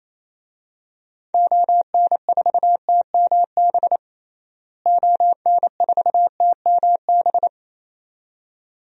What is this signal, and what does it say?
Morse code, heard as a keyed tone, about 28 wpm, 700 Hz. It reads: ON4TMB ON4TMB
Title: DXCC-100 Call Signs - 6 Characters - Repeated 2x - 28wpm